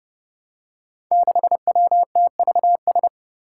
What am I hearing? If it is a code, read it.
Morse code, heard as a keyed tone, about 30 wpm, 700 Hz. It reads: 6WTVS
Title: Five Challenging Letter-Number Combinations 30wpm